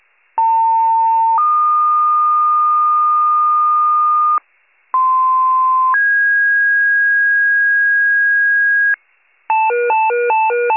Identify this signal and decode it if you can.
Two-tone page, heard as a one-second tone, then a three-second tone, about 900 then 1220 Hz, about 1040 then 1650 Hz, with noise. It is heard from 0.4 to 8.9 s.